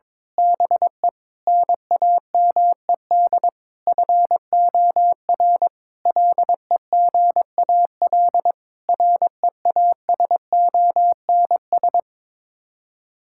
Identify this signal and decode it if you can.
Morse code, heard as a keyed tone, about 22 wpm, 700 Hz. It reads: BE NAMED FOR LEGAL REASONS
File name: practice / 22wpm / Sets of 8 Words - 22wpm